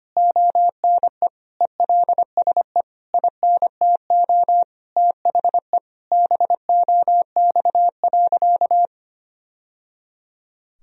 Morse code, heard as a keyed tone, about 25 wpm, 700 Hz. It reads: ONE ELSE INTO THE BOX.
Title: Sentences from Top 1000 Words - Review Part 2 - 25wpm